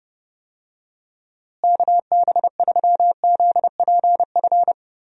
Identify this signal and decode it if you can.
Morse code, heard as a keyed tone, about 31 wpm, 700 Hz. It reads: KB3ZPF
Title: USA Call Signs - 6 Characters - Repeated 3x - 30wpm